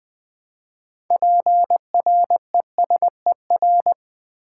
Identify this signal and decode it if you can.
Morse code, heard as a keyed tone, about 20 wpm, 700 Hz. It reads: PRESER
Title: Build A Word - 2 Letters At A Time - 8-Letter Words - 20wpm